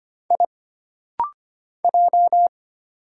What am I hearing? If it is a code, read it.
Morse code, heard as a keyed tone, about 25 wpm, 700 Hz. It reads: I J
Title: Single Letters - Rapid-Fire 25wpm